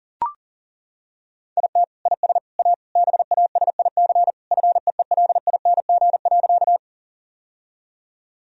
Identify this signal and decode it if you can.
Morse code, heard as a keyed tone, about 40 wpm, 700 Hz. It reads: IT IS A BASIC FEELING.
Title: Sentences from Top 1000 Words - Review Part 3 - 40wpm